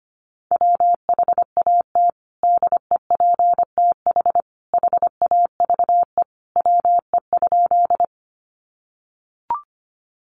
Morse code, heard as a keyed tone, about 25 wpm, 700 Hz. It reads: WHAT DEPTH HAVE WE?
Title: Sentences from Top 1200 Words 25wpm